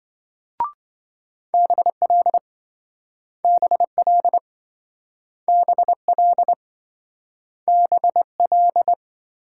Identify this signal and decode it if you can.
Morse code, heard as a keyed tone, about 27 wpm, 700 Hz. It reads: BL BL BL BL